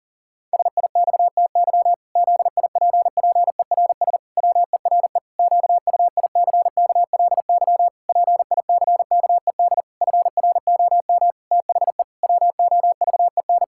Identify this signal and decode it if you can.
Morse code, heard as a keyed tone, about 37 wpm, 700 Hz. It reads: SIXTY ZIPPERS WERE QUICKLY PICKED FROM THE WOVEN
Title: Pangram Challenge v2 - 40wpm